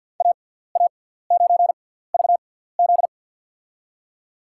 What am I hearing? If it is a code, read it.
Morse code, heard as a keyed tone, about 47 wpm, spaced as for 17 wpm, 700 Hz. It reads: AA9VZ
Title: USA Call Signs - 6x Character Spacing - 50wpm